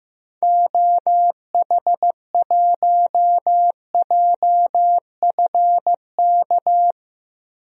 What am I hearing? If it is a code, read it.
Morse code, heard as a keyed tone, about 15 wpm, 700 Hz. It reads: OH1JFK
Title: DXCC-100 Call Signs - 6 Characters - 15wpm